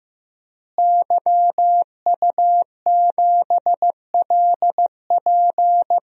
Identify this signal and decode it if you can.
Morse code, heard as a keyed tone, about 15 wpm, 700 Hz. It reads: YU7LP
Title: DXCC-100 Call Signs - 5 Characters - Repeated 2x - 15wpm